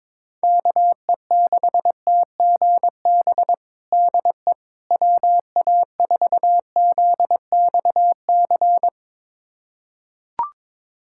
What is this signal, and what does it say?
Morse code, heard as a keyed tone, about 22 wpm, 700 Hz. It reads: KE6TGB DE WA4ZXC
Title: USA Callsign DE Callsign 22wpm